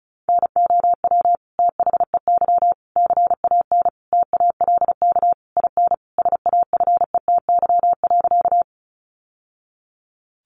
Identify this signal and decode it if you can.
Morse code, heard as a keyed tone, about 34 wpm, 700 Hz. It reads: NOW THEY CAN TALK IN SAFETY.
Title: Sentences from Top 700 Words 35wpm